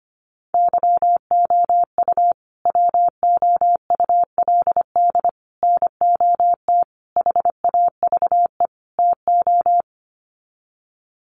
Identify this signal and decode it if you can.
Morse code, heard as a keyed tone, about 25 wpm, 700 Hz. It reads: YOU WOULD NOT HAVE TO
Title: Sets of 5 Words Encore 25wpm